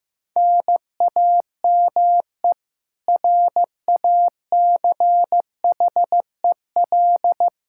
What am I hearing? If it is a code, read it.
Morse code, heard as a keyed tone, about 15 wpm, 700 Hz. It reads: NAME RACHEL
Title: Practice QSO - Exchange 1 - Beginning - 15wpm